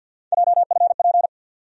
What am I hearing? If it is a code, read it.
Morse code, heard as a keyed tone, about 48 wpm, 690 Hz. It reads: JFP
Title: USA Call Sign Postfixes - 3 Letters 50wpm